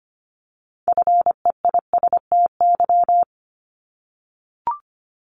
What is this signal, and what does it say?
Morse code, heard as a keyed tone, about 25 wpm, 700 Hz. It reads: FEISTY
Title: Multi-letter Phonemes 25wpm